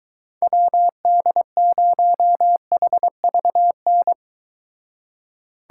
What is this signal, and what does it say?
Morse code, heard as a keyed tone, about 22 wpm, 700 Hz. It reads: WD0HVN